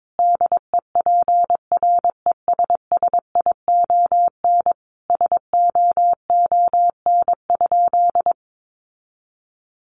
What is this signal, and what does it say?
Morse code, heard as a keyed tone, about 22 wpm, 700 Hz. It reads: DEPRESSION SOON?
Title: Sentences from Top 1300 Words 22wpm